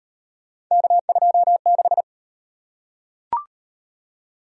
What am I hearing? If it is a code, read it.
Morse code, heard as a keyed tone, about 32 wpm, 700 Hz. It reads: K26